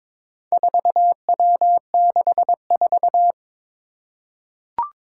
Morse code, heard as a keyed tone, about 23 wpm, 700 Hz. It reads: 4W64
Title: Four Challenging Letter-Number Combinations 22wpm